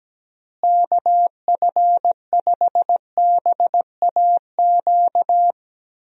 Morse code, heard as a keyed tone, about 17 wpm, 700 Hz. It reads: KF5BAQ
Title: USA Call Signs - 6 Characters - Repeated 2x - 17wpm